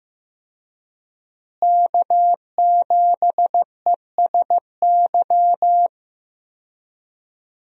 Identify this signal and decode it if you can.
Morse code, heard as a keyed tone, about 15 wpm, 700 Hz. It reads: K7ESY
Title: USA Call Signs - 5 Characters - Repeated 2x - 15wpm